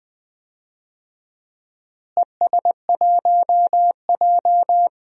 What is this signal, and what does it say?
Morse code, heard as a keyed tone, about 20 wpm, 700 Hz. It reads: ES1J
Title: DXCC-100 Call Signs - 4 Characters - Repeated 2x - 20wpm